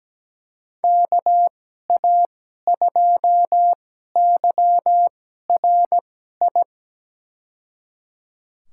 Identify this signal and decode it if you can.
Morse code, heard as a keyed tone, about 17 wpm, spaced as for 12 wpm, 700 Hz. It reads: KA2YRI
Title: USA Call Signs - 2x Character Spacing - 17wpm